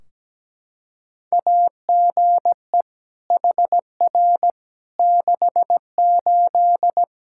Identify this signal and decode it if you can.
Morse code, heard as a keyed tone, about 17 wpm, 700 Hz. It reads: AGE HR 68
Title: K1USN SST QSO Practice 17wpm